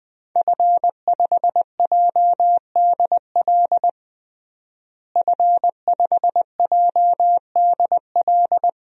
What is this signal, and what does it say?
Morse code, heard as a keyed tone, about 21 wpm, 700 Hz. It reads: F5JDL F5JDL
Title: DXCC-100 Call Signs - 5 Characters - Repeated 2x - 20wpm